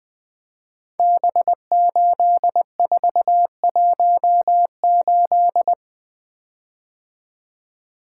Morse code, heard as a keyed tone, about 20 wpm, 700 Hz. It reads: B8418
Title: Code Groups - 5 Letter:Numbers - 20wpm